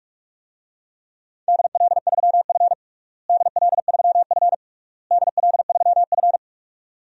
Morse code, heard as a keyed tone, about 46 wpm, 690 Hz. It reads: DL3F DL3F DL3F
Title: DXCC-100 Call Signs - 4 Characters - Repeated 3x - 45wpm